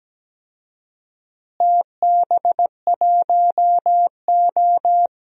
Morse code, heard as a keyed tone, about 17 wpm, 700 Hz. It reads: TB1O
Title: DXCC-100 Call Signs - 4 Characters - Repeated 3x - 17wpm